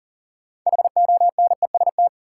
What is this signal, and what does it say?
Morse code, heard as a keyed tone, about 40 wpm, 700 Hz. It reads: HONEST
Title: Sets of 5 Words Encore - New Word Drill - 40wpm